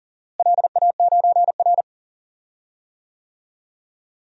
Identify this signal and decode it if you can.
Morse code, heard as a keyed tone, about 40 wpm, 700 Hz. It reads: LA9R
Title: DXCC-100 Call Signs - 4 Characters - Repeated 2x - 40wpm